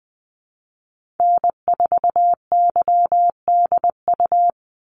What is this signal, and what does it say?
Morse code, heard as a keyed tone, about 20 wpm, 700 Hz. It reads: N4YDU
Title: Call Signs Repeated 3x - 20wpm